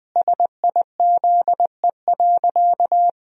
Morse code, heard as a keyed tone, about 20 wpm, 700 Hz. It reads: SIZE.
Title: 2x Word-Spacing - Sentences from Top 400 Words - 20wpm